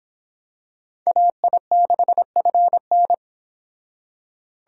Morse code, heard as a keyed tone, about 25 wpm, 700 Hz. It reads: AI6FN